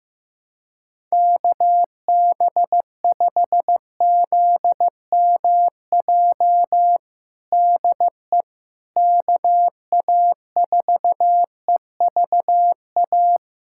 Morse code, heard as a keyed tone, about 15 wpm, 700 Hz. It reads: KB5ZMJ DE KA4EVA
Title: USA Callsign DE Callsign - 6 Characters - 15wpm